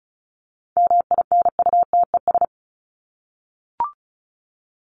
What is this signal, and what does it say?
Morse code, heard as a keyed tone, about 34 wpm, 700 Hz. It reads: MINUTES
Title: ES:S Suffix - Top 100 Words with Base Words 35wpm